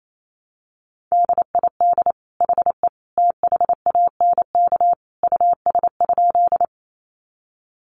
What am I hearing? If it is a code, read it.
Morse code, heard as a keyed tone, about 28 wpm, 700 Hz. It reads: DID HE THANK US?
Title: Sentences from Top 600 Words 28wpm